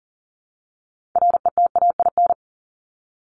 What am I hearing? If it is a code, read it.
Morse code, heard as a keyed tone, about 40 wpm, 700 Hz. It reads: RETAIN
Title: Top 1751-2000 Words 40wpm